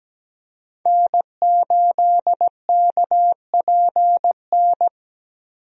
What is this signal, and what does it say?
Morse code, heard as a keyed tone, about 17 wpm, 700 Hz. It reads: N8KPN
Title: USA Call Signs - 5 Characters - Repeated 3x - 17wpm